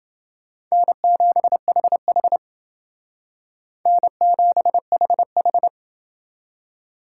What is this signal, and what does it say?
Morse code, heard as a keyed tone, about 29 wpm, 700 Hz. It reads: N7HH N7HH